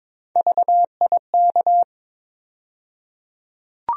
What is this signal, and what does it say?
Morse code, heard as a keyed tone, about 22 wpm, 700 Hz. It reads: VIK
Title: USA Call Sign Postfixes - 3 Letters 22wpm